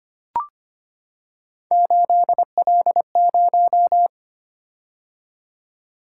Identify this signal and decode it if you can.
Morse code, heard as a keyed tone, about 25 wpm, 700 Hz. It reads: 8L0
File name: Three Letter-Number Combinations - 25wpm